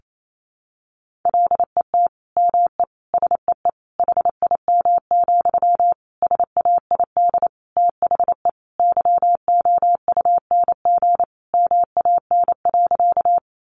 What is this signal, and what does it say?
Morse code, heard as a keyed tone, about 28 wpm, 700 Hz. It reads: LET ME SEE HIM, SAID THE YOUNG MAN.
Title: Sentences from Top 400 Words 28wpm